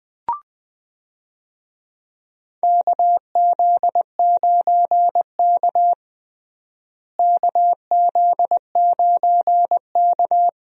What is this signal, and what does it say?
Morse code, heard as a keyed tone, about 20 wpm, 700 Hz. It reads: KZ9K KZ9K
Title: USA Call Signs - 4 Characters - Repeated 3x - 20wpm